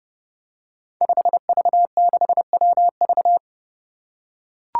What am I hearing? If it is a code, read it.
Morse code, heard as a keyed tone, about 32 wpm, 700 Hz. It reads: 5V6WV
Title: Five Challenging Letter-Number Combinations 30wpm